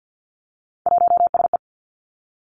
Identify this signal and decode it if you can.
Morse code, heard as a keyed tone, about 50 wpm, 700 Hz. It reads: 1SE
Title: Three Challenging Letter-Number Combinations 50wpm